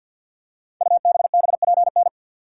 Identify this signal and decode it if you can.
Morse code, heard as a keyed tone, about 47 wpm, 700 Hz. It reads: UBBPN
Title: Code Groups - 5 Letter:Numbers - 50wpm